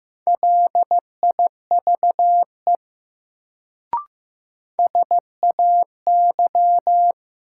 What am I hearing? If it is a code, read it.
Morse code, heard as a keyed tone, about 15 wpm, 700 Hz. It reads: LIVE SAY
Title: Sentences from Top 200 Words - New Word Drill - 15wpm